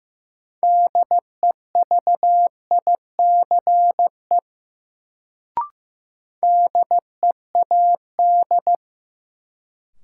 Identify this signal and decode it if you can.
Morse code, heard as a keyed tone, about 15 wpm, 700 Hz. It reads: DEVICE DEAD
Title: Sentences from Top 1500 Words - New Word Drill Alt - 15wpm